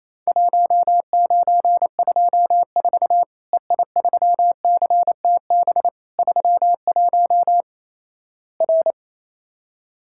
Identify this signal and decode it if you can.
Morse code, heard as a keyed tone, about 28 wpm, 700 Hz. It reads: 1924 EI3CTB 31 R
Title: Worked All Europe DX Contest - QTC Exchange - 28wpm